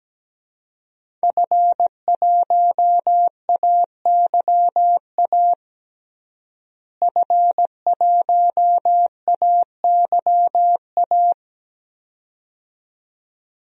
Morse code, heard as a keyed tone, about 17 wpm, 700 Hz. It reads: F1AYA F1AYA
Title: DXCC-100 Call Signs - 5 Characters - Repeated 2x - 17wpm